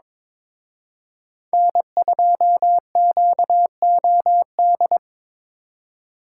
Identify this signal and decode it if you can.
Morse code, heard as a keyed tone, about 26 wpm, 700 Hz. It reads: N2QOD
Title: Speed Racing - USA 1x3 Call Signs 22wpm